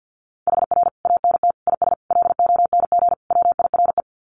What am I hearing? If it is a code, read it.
Morse code, heard as a keyed tone, about 48 wpm, 700 Hz. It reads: HR ANT IS LONG WIRE
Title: Common QSO Phrases 50wpm